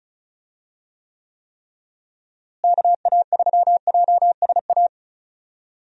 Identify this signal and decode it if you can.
Morse code, heard as a keyed tone, about 34 wpm, 700 Hz. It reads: KA3JSA
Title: USA Call Signs - 6 Characters - Repeated 3x - 35wpm